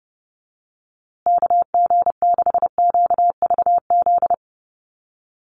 Morse code, heard as a keyed tone, about 30 wpm, 700 Hz. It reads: KG6QVZ
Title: USA Call Signs Repeated 3x 30wpm